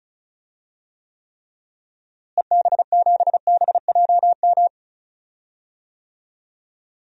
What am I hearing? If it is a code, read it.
Morse code, heard as a keyed tone, about 34 wpm, 700 Hz. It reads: EB7BJM
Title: DXCC-100 Call Signs - 6 Characters - Repeated 3x - 35wpm